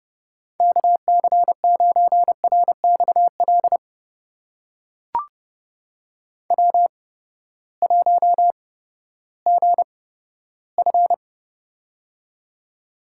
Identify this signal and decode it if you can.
Morse code, heard as a keyed tone, about 30 wpm, spaced as for 15 wpm, 700 Hz. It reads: KC9RXL W1GF
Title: USA Call Signs - 8x Character Spacing - 30wpm